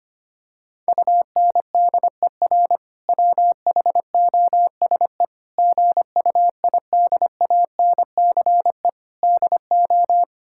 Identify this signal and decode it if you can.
Morse code, heard as a keyed tone, about 25 wpm, 700 Hz. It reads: UNDER WHOSE GUIDANCE DO
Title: Sentences from Top 1600 Words 25wpm